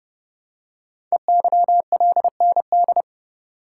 Morse code, heard as a keyed tone, about 30 wpm, 690 Hz. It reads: EYLND
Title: Code Groups - 5 Letters - 30wpm